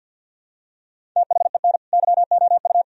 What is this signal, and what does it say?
Morse code, heard as a keyed tone, about 46 wpm, 700 Hz. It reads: THEN YOU
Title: Sentences from Top 1000 Words - Review Part 1 - 50wpm